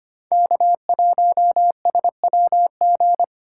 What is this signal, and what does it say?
Morse code, heard as a keyed tone, about 25 wpm, 700 Hz. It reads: K1SWG
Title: USA Call Signs - 5 Characters - 2x Character Spacing - 25wpm